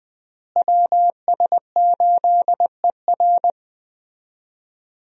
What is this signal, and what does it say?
Morse code, heard as a keyed tone, about 20 wpm, 700 Hz. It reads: WS8ER
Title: USA Call Signs - 5 Characters - Repeated 3x - 20wpm